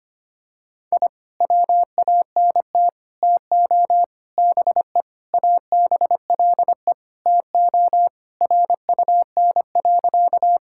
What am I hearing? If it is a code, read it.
Morse code, heard as a keyed tone, about 25 wpm, 700 Hz. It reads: I WANT TO BE ABLE TO RUN.
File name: practice / 25wpm / Sentences with 3-Word Phrases - 25wpm